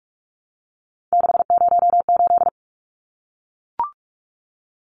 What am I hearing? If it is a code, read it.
Morse code, heard as a keyed tone, about 44 wpm, 700 Hz. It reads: 608
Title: Numbers - 3-Digit 45wpm